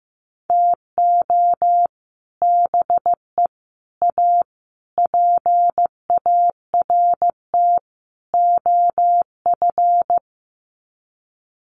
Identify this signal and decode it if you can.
Morse code, heard as a keyed tone, about 15 wpm, 700 Hz. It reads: TO BE A PART OF
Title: Sets of 5 Words 15wpm